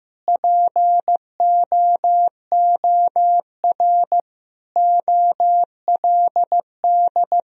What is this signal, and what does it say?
Morse code, heard as a keyed tone, about 16 wpm, 700 Hz. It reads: POOR OLD
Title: Sentences from Top 1000 Words - Review Part 2 - 15wpm